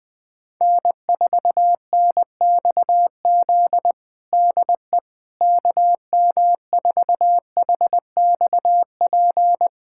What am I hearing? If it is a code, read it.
Morse code, heard as a keyed tone, about 20 wpm, 700 Hz. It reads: N4NXZ DE KM4HXP
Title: USA Callsign DE Callsign 20wpm